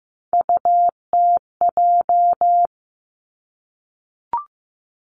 Morse code, heard as a keyed tone, about 15 wpm, 700 Hz. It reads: UTJ
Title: Three Challenging Letter-Number Combinations 15wpm